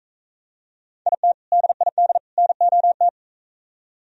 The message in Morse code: IT DID NOT

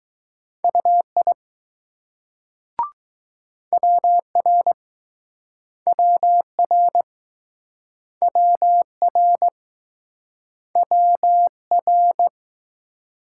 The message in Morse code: UI WR WR WR WR

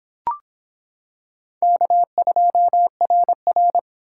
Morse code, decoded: K2RR

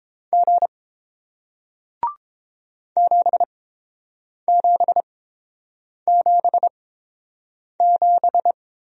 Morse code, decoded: G 7777

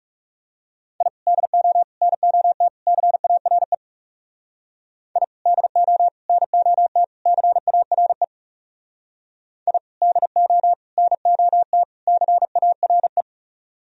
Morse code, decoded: I DO NOT CARE I DO NOT CARE I DO NOT CARE

